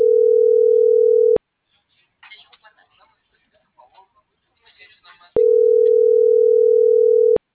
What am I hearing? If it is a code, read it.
Ringback tone (North America), 440 + 480 Hz.
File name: CENTRAL MULTIMODAL / out josimar.cruz - .79328